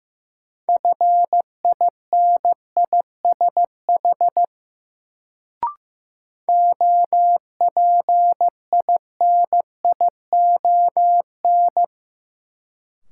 Morse code, heard as a keyed tone, about 15 wpm, 700 Hz. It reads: FINISH OPINION
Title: Sentences from Top 1200 Words - New Word Drill Alt - 15wpm